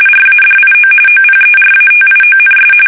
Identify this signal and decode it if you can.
RTTY, about 45.45 baud. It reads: RADIO OPERATIONS
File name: fsk850 / teletype-RTTY